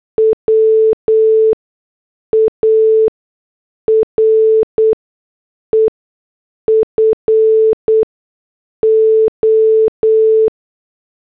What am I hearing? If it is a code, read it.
Morse code, heard as a keyed tone, about 8 wpm, spaced as for 6 wpm, 440 Hz. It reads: WARE FO
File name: morse / headlines